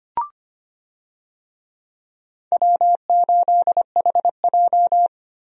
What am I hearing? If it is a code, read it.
Morse code, heard as a keyed tone, about 25 wpm, 700 Hz. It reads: W8HJ
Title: USA Call Signs - 4 Characters - Repeated 2x - 25wpm